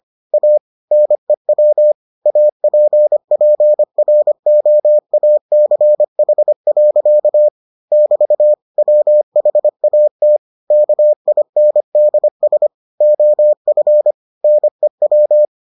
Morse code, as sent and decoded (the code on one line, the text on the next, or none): .- / -. .-- / .- .--. .--. .-. --- .- -.-. .... .-.-.- / -...- / .-- .... .- - / -.- .. -. -.. ... / --- ..-. / -. .--
A NEW APPROACH. <BT> WHAT KINDS OF NEW